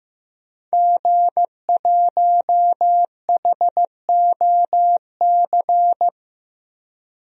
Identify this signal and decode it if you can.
Morse code, heard as a keyed tone, about 15 wpm, 700 Hz. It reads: G1HOC
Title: DXCC-100 Call Signs - 5 Characters - 15wpm